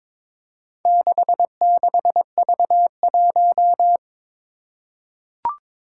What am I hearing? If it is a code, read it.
Morse code, heard as a keyed tone, about 22 wpm, 700 Hz. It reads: 66V1
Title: Four Challenging Letter-Number Combinations 22wpm